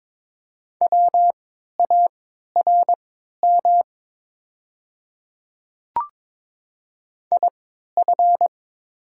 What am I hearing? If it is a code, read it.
Morse code, heard as a keyed tone, about 22 wpm, spaced as for 13 wpm, 690 Hz. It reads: WARM IF
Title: 3x Character Spacing - Top 500 Words - 22wpm